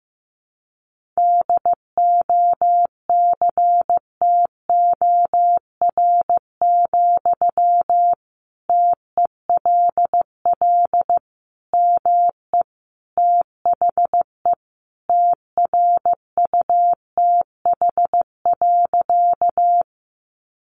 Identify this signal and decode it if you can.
Morse code, heard as a keyed tone, about 15 wpm, 700 Hz. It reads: DOCTOR, TELL ME THE TRUTH.
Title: Sentences from Top 1500 Words 15wpm